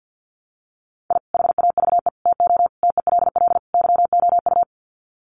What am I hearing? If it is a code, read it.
Morse code, heard as a keyed tone, about 50 wpm, 700 Hz. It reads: I HAVE TO TELL YOU
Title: Sets of 5 Words 50wpm